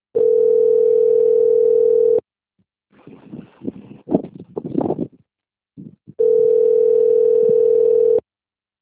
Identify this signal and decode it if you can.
Ringback tone (North America), 440 + 480 Hz.